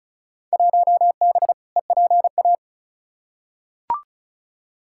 Morse code, heard as a keyed tone, about 36 wpm, 700 Hz. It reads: 1B EPA
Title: Field Day Exchanges Encore - 35wpm